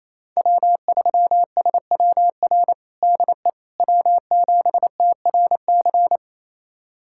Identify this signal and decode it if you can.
Morse code, heard as a keyed tone, about 28 wpm, 700 Hz. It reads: W3SWR DE W7TRC
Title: USA Callsign DE Callsign - 5 Characters - 28wpm